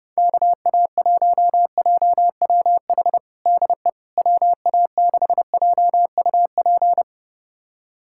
Morse code, heard as a keyed tone, about 32 wpm, 700 Hz. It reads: KA1JWH DE WA6JUP
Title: USA Callsign DE Callsign - 6 Characters - 30wpm